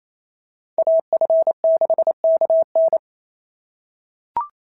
Morse code, heard as a keyed tone, about 28 wpm, 690 Hz. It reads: AF6KN
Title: SOTA QSO Practice 28wpm